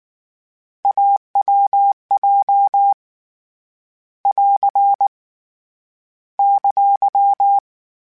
Morse code, heard as a keyed tone, about 19 wpm, 800 Hz. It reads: AWJ <AR> !
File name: Difficult Combos / Difficult Combos-mp3s / AWJ ShortRun@18